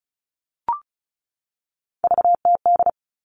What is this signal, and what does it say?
Morse code, heard as a keyed tone, about 34 wpm, 700 Hz. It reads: VTD
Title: All Cut Numbers - 3-Digit 35wpm